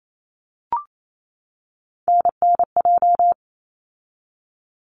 Morse code, heard as a keyed tone, about 28 wpm, 700 Hz. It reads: NNJ
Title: ARRL Sections 28wpm